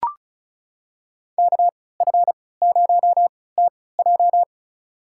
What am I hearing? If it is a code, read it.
Morse code, heard as a keyed tone, about 34 wpm, spaced as for 20 wpm, 700 Hz. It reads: KF0TJ
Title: USA Call Signs - 5 Characters - 3x Character Spacing - 35wpm